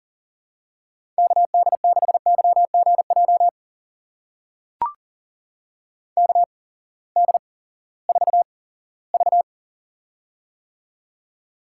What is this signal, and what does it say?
Morse code, heard as a keyed tone, about 40 wpm, spaced as for 20 wpm, 700 Hz. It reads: KD6YGJ KD4V
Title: USA Call Signs - 8x Character Spacing - 40wpm